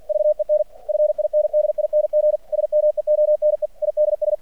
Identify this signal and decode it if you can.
Morse code, heard as a keyed tone, about 50 wpm, 600 Hz. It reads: VEN FINDING SMEONE IDS
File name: archived / CRSnet